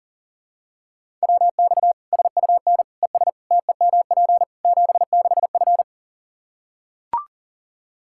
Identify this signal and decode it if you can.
Morse code, heard as a keyed tone, about 40 wpm, 700 Hz. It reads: WX SUN ES TEMP 76F
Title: K1USN SST QSO Practice 40wpm